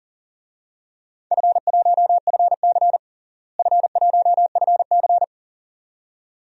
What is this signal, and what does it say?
Morse code, heard as a keyed tone, about 37 wpm, 700 Hz. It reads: F1FC F1FC
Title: DXCC-100 Call Signs - 4 Characters - Repeated 2x - 40wpm